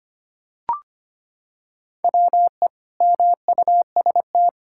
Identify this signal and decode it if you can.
Morse code, heard as a keyed tone, about 25 wpm, 700 Hz. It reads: WE MUST
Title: Sentences from Top 1600 Words 25wpm